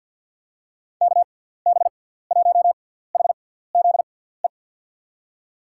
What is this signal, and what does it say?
Morse code, heard as a keyed tone, about 53 wpm, spaced as for 17 wpm, 700 Hz. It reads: KB1HZE
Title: USA Call Signs - 6 Characters - 6x Character Spacing - 50wpm